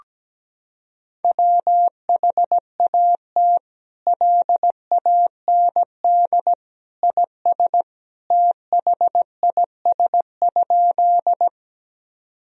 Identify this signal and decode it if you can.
Morse code, heard as a keyed tone, about 17 wpm, 700 Hz. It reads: WHAT LAND IS THIS?
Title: Sentences from Top 1000 Words - Review Part 1 - 17wpm